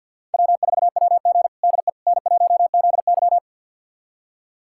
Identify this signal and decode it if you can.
Morse code, heard as a keyed tone, about 47 wpm, 700 Hz. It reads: W4WG DE N1ZY
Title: USA Callsign DE Callsign - 4 Characters - 50wpm